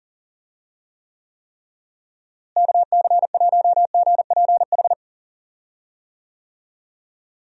Morse code, heard as a keyed tone, about 40 wpm, 700 Hz. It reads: KC1GPH
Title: USA Call Signs - 6 Characters - Repeated 3x - 40wpm